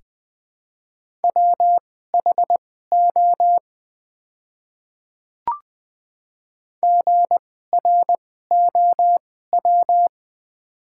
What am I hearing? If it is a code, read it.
Morse code, heard as a keyed tone, about 20 wpm, spaced as for 14 wpm, 700 Hz. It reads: WHO GROW
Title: 2x Character Spacing - Top 300 Words - 20wpm